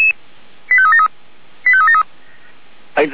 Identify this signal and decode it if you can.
Five-tone selective call (ZVEI), heard as quick runs of five tones, tones of about 70 ms, with speech. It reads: 85373 85373